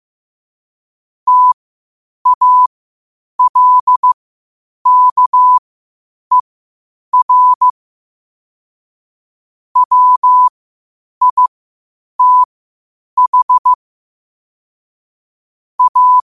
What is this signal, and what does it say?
Morse code, heard as a keyed tone, about 15 wpm, spaced as for 8 wpm, 1000 Hz. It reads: TALKER WITH A
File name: qotdfiles / 8wpm